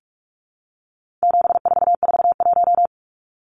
Morse code, heard as a keyed tone, about 44 wpm, 700 Hz. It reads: 7441
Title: Numbers - 4-Digit 45wpm